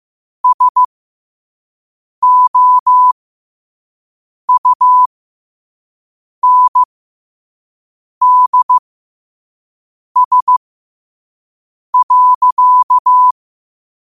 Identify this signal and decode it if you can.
Morse code, heard as a keyed tone, about 15 wpm, spaced as for 5 wpm, 1000 Hz. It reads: SOUNDS.
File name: qotdfiles / 5wpm